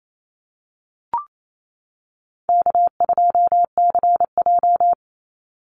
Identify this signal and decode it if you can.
Morse code, heard as a keyed tone, about 28 wpm, 700 Hz. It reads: K2CJ
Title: Call Signs - 28wpm